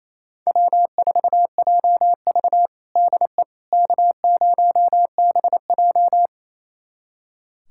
Morse code, heard as a keyed tone, about 28 wpm, 700 Hz. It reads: W4JV DE K0BJ
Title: USA Callsign DE Callsign - 4 Characters - 28wpm